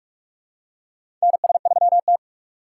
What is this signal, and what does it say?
Morse code, heard as a keyed tone, about 45 wpm, 700 Hz. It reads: NS3T
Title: USA Call Signs - 4 Characters - 4x Character Spacing - 45wpm